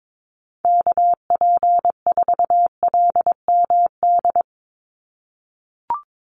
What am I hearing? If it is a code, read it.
Morse code, heard as a keyed tone, about 22 wpm, 700 Hz. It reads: KP4LMD
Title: USA Call Signs 22wpm